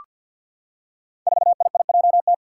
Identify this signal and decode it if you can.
Morse code, heard as a keyed tone, about 47 wpm, 700 Hz. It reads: 4IIJT
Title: Five Challenging Letter-Number Combinations 50wpm